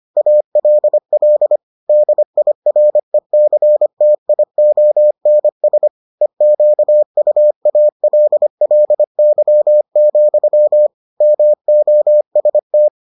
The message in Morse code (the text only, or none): ALL DIRECTIONS EQUALLY, MOST